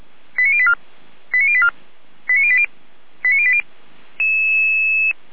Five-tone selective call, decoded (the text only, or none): 89084 89084 89080 89080